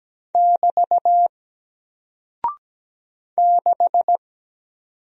Morse code, heard as a keyed tone, about 17 wpm, 700 Hz. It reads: <BT> 6
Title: Single Character (Letter:Number:Punctuation:Prosign) Alt 17wpm